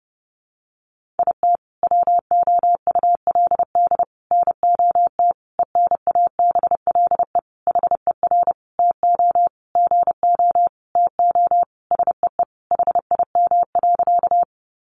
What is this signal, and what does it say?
Morse code, heard as a keyed tone, about 30 wpm, 700 Hz. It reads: IT WOULD NOT ENABLE HER TO GO TO SEE HIM.
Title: Sentences from Top 1500 Words 30wpm